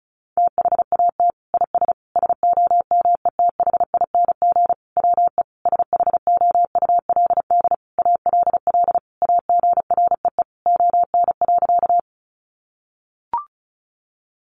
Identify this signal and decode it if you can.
Morse code, heard as a keyed tone, about 34 wpm, 700 Hz. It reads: THAT IS SOMETHING WE SHOULD ALL AGREE ON.
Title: Sentences from Top 1100 Words 35wpm